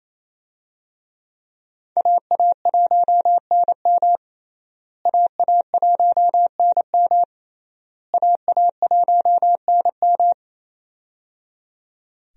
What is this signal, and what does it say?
Morse code, heard as a keyed tone, about 27 wpm, 700 Hz. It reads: AA1NM AA1NM AA1NM
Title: USA Call Signs - 5 Characters - Repeated 3x - 28wpm